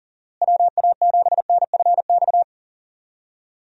Morse code, heard as a keyed tone, about 40 wpm, 700 Hz. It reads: WA7NFX
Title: USA Call Signs - 6 Characters - Repeated 2x - 40wpm